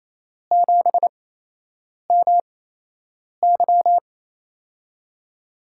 Morse code, heard as a keyed tone, about 28 wpm, spaced as for 8 wpm, 700 Hz. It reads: 7MY
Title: USA Call Signs - 5 Characters - 8x Character Spacing - 28wpm